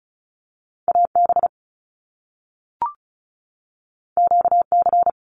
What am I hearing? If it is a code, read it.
Morse code, heard as a keyed tone, about 36 wpm, 700 Hz. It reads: AB QC
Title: Canadian Provinces and Territories 35wpm